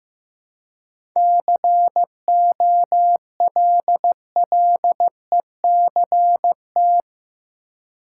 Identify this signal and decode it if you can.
Morse code, heard as a keyed tone, about 15 wpm, 700 Hz. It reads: COLLECT
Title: Seven Letter Words - 15wpm